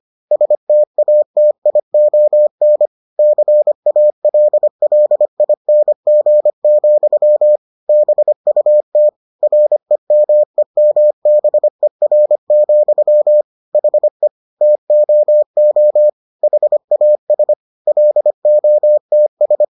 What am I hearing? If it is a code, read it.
Morse code, heard as a keyed tone, about 25 wpm, 600 Hz. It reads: STATION CALLING, BUT REMEMBER, HE TOO HAS LOTS